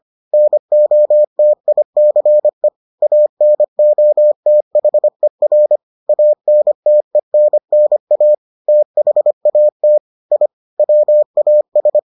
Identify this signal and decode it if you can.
Morse code, heard as a keyed tone, about 25 wpm, 600 Hz. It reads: NOTICE ANOTHER ANTENNA THAT I WAS